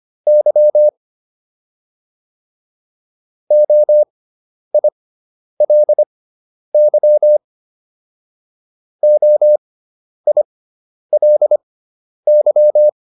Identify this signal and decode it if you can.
Morse code, heard as a keyed tone, about 25 wpm, spaced as for 10 wpm, 600 Hz. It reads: Y OILY OILY